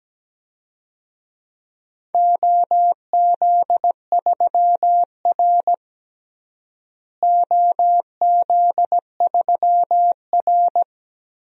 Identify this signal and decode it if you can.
Morse code, heard as a keyed tone, about 17 wpm, 700 Hz. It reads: OZ3R OZ3R
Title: DXCC-100 Call Signs - 4 Characters - Repeated 3x - 17wpm